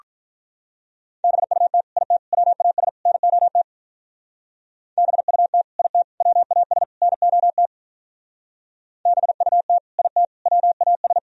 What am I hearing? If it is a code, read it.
Morse code, heard as a keyed tone, about 42 wpm, 700 Hz. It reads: BUT IT WAS NOT BUT IT WAS NOT BUT IT WAS
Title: Speed-Racing - Sets of 4 Words 35wpm